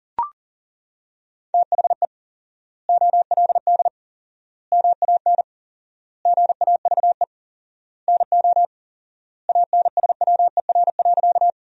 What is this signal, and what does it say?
Morse code, heard as a keyed tone, about 40 wpm, 700 Hz. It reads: THE OLD MAN GAVE NO ANSWER.
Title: 4x Word-Spacing - Sentences from Top 500 Words - 40wpm